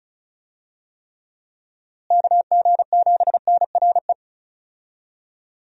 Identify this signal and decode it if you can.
Morse code, heard as a keyed tone, about 36 wpm, 700 Hz. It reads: KG7NRE
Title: USA Call Signs - 6 Characters - Repeated 3x - 35wpm